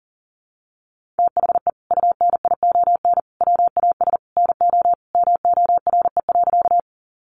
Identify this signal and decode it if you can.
Morse code, heard as a keyed tone, about 40 wpm, 700 Hz. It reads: THE UNION WAS NO MORE.
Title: Sentences from Top 1400 Words 40wpm